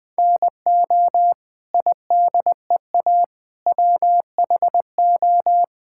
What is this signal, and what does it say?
Morse code, heard as a keyed tone, about 20 wpm, 700 Hz. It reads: NO IDEA WHO
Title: Sets of 9 Words - 20wpm